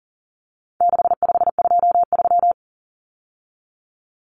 Morse code, heard as a keyed tone, about 40 wpm, 700 Hz. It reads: 6523
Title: Numbers - 4-Digit 40wpm